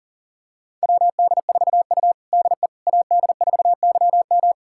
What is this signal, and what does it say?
Morse code, heard as a keyed tone, about 40 wpm, 700 Hz. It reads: WD4U DE AD4YM
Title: USA Callsign DE Callsign 40wpm